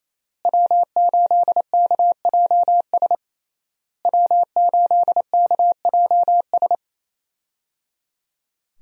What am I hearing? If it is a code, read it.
Morse code, heard as a keyed tone, about 28 wpm, 700 Hz. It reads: W8KJS W8KJS
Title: USA Call Signs - 5 Characters - Repeated 3x - 28wpm